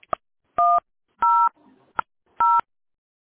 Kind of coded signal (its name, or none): DTMF